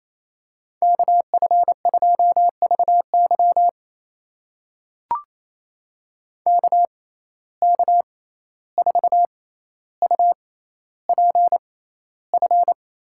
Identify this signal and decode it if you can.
Morse code, heard as a keyed tone, about 28 wpm, spaced as for 14 wpm, 700 Hz. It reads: KF2VY KK4UPF
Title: USA Call Signs - 6x Character Spacing - 28wpm